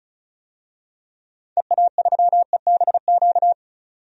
Morse code, heard as a keyed tone, about 36 wpm, 700 Hz. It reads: EA3EBQ
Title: DXCC-100 Call Signs - 6 Characters - 35wpm